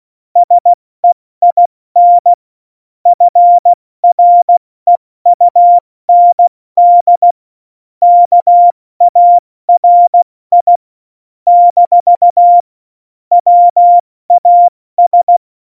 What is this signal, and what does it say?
Morse code, heard as a keyed tone, about 16 wpm, 700 Hz. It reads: SEIN FREUND KARI - WAS
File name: Morsekurs / Witze / Witz 17. Oktober 17, Tempo 70